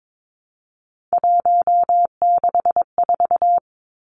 Morse code, heard as a keyed tone, about 22 wpm, 700 Hz. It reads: 164
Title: Numbers - 3-Digit 22wpm